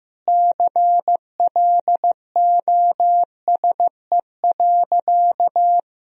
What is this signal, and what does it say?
Morse code, heard as a keyed tone, about 15 wpm, 700 Hz. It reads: CLOSE.
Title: Sentences from Top 1000 Words - Review Part 1 - 15wpm